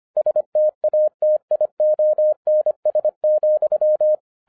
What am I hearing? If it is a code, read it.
Morse code, heard as a keyed tone, about 25 wpm, 610 Hz. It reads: STATIONS,
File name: 2011-10-mp3 / n09-CW TO PS2 CONVERTER 25wpm